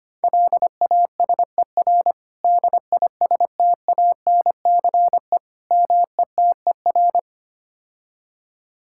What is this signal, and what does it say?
Morse code, heard as a keyed tone, about 25 wpm, 700 Hz. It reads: LASER DISTANCE METER
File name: practice / 25wpm / Difficult Words - Speed-Racing with Hints - 25wpm